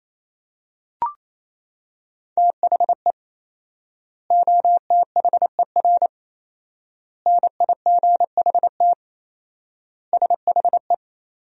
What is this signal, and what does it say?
Morse code, heard as a keyed tone, about 27 wpm, 700 Hz. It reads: THE OTHER NIGHT SHE
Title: 4x Word-Spacing - Sentences from Top 300 Words - 28wpm